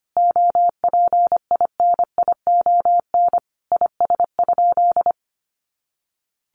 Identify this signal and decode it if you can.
Morse code, heard as a keyed tone, about 25 wpm, 700 Hz. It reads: OPINION IS?
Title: Sentences from Top 1200 Words 25wpm